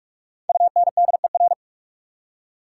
Morse code, heard as a keyed tone, about 61 wpm, spaced as for 53 wpm, 690 Hz. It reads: UNDER
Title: Speed-Racing Prefixes and Suffixes 45wpm